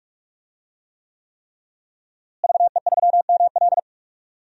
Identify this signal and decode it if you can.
Morse code, heard as a keyed tone, about 46 wpm, 700 Hz. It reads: VE3ML
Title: DXCC-100 Call Signs - 5 Characters - Repeated 3x - 45wpm